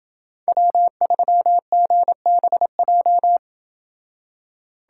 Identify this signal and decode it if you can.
Morse code, heard as a keyed tone, about 26 wpm, 700 Hz. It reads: W3GBJ